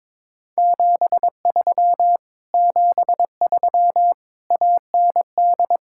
Morse code, heard as a keyed tone, about 22 wpm, 700 Hz. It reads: 73 73 AND
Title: Speed-Racing Common QSO Phrases - 22wpm